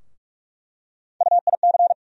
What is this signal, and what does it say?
Morse code, heard as a keyed tone, about 45 wpm, 700 Hz. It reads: UIC
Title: USA Call Sign Postfixes - 3 Letters 45wpm